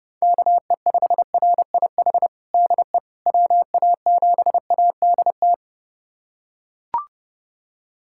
Morse code, heard as a keyed tone, about 29 wpm, 700 Hz. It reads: KE5RIH DE WA7ADT
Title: USA Callsign DE Callsign - 6 Characters - 30wpm